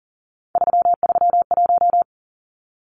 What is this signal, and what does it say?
Morse code, heard as a keyed tone, about 40 wpm, 700 Hz. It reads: 331
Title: Numbers - 3-Digit 40wpm